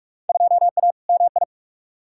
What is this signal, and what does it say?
Morse code, heard as a keyed tone, about 43 wpm, 700 Hz. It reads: 2A MI
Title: Field Day Exchanges - 45wpm